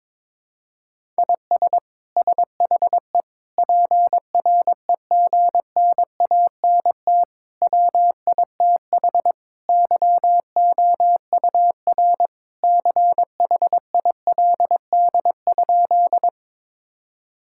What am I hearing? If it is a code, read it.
Morse code, heard as a keyed tone, about 22 wpm, 700 Hz. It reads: IS SHE PREGNANT WITH YOUR CHILD?
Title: Sentences from Top 1000 Words - Review Part 6 - 22wpm